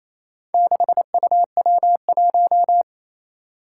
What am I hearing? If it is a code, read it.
Morse code, heard as a keyed tone, about 29 wpm, 700 Hz. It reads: 6UW1
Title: Four Challenging Letter-Number Combinations 28wpm